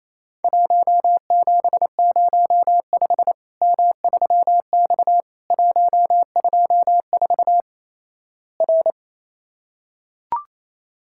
Morse code, heard as a keyed tone, about 28 wpm, 700 Hz. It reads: 1705 M3X 124 R
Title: Worked All Europe DX Contest - QTC Exchange - 28wpm